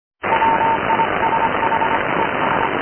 Morse code, heard as a keyed tone, about 25 wpm, 910 Hz, with radio noise. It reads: WISH ES
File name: noisecanceller fichiers / noise cw